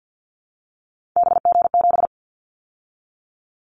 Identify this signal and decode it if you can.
Morse code, heard as a keyed tone, about 50 wpm, 700 Hz. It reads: BG7
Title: Callsign Prefixes for Top 100 DXCC - 50wpm